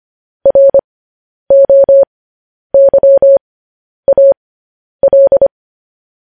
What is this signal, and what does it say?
Morse code, heard as a keyed tone, about 25 wpm, spaced as for 10 wpm, 550 Hz. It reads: ROYAL
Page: News Headlines in Morse Code at 10 WPM